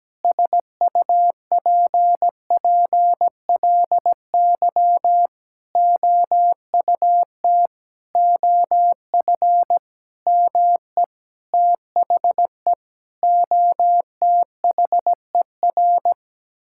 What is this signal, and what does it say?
Morse code, heard as a keyed tone, about 17 wpm, 700 Hz. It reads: SUPPLY OUT OF ME THE OTHER
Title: Sentences from Top 1000 Words - Review Part 5 - 17wpm